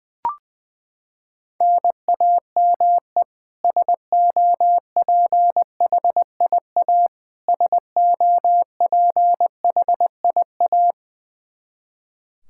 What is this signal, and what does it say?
Morse code, heard as a keyed tone, about 21 wpm, 700 Hz. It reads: NAME SOPHIA SOPHIA
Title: Practice QSO - Exchange 1 - Beginning - 20wpm